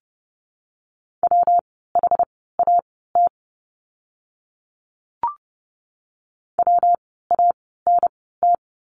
Morse code, heard as a keyed tone, about 30 wpm, spaced as for 17 wpm, 700 Hz. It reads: WHAT WANT
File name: course / 30:10wpm / 230 Lesson 37f - Letter x - Words 30wpm